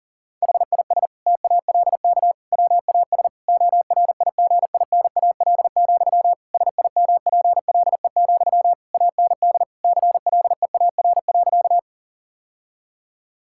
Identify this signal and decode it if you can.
Morse code, heard as a keyed tone, about 37 wpm, 700 Hz. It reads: HIS TALK WAS ORIGINAL, SIMPLE, AND CLEAR.
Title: Sentences from Top 1000 Words - Review Part 6 - 40wpm